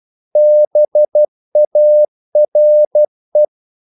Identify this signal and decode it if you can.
Morse code, heard as a keyed tone, about 12 wpm, 600 Hz. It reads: BARE
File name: Korte ord DK 12wpm